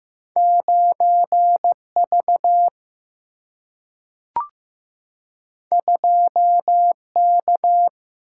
Morse code, heard as a keyed tone, about 15 wpm, 700 Hz. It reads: 9V 2K
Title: Single-Letter Single-Number 15wpm